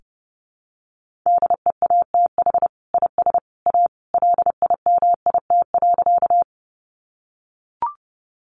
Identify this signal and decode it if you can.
Morse code, heard as a keyed tone, about 30 wpm, 700 Hz. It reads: DEATH IS A LIMIT.
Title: Sentences from Top 1500 Words 30wpm